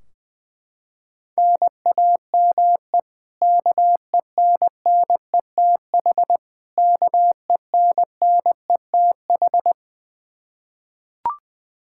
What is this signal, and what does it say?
Morse code, heard as a keyed tone, about 21 wpm, 700 Hz. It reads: NAME KENNETH KENNETH
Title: Practice QSO - Exchange 1 - Beginning - 20wpm